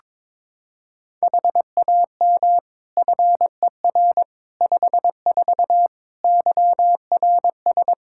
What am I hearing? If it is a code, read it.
Morse code, heard as a keyed tone, about 22 wpm, 700 Hz. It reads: HAM FER 54 YRS
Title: K1USN SST QSO Practice 22wpm